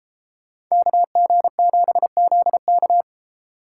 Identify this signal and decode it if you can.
Morse code, heard as a keyed tone, about 37 wpm, 700 Hz. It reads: KG7ZK